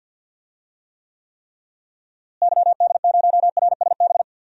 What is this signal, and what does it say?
Morse code, heard as a keyed tone, about 50 wpm, 700 Hz. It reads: YD0RSB
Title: DXCC-100 Call Signs - 6 Characters - Repeated 3x - 50wpm